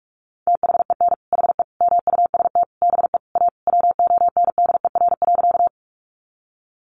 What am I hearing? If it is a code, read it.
Morse code, heard as a keyed tone, about 44 wpm, 700 Hz. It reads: THEN HE MUST BE A WONDER.
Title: Sentences from Top 500 Words 45wpm